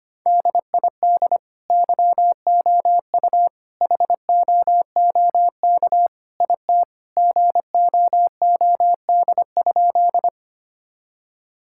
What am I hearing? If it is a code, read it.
Morse code, heard as a keyed tone, about 25 wpm, 700 Hz. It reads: DID YOU HOOK IT GOOD?
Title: Sentences from Top 1600 Words 25wpm